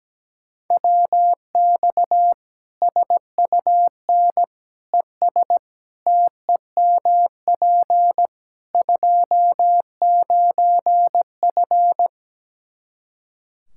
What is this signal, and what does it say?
Morse code, heard as a keyed tone, about 17 wpm, 700 Hz. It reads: WX SUN ES TEMP 29F
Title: Practice QSO - Exchange 2 - Info - 17wpm